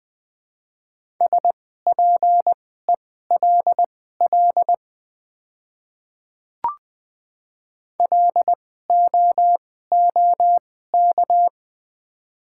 Morse code, heard as a keyed tone, about 20 wpm, spaced as for 14 wpm, 700 Hz. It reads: SPELL LOOK
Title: 2x Character Spacing - Top 300 Words - 20wpm